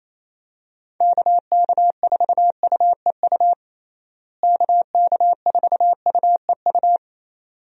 Morse code, heard as a keyed tone, about 27 wpm, 700 Hz. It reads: KK4UEU KK4UEU
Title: USA Call Signs - 6 Characters - Repeated 2x - 28wpm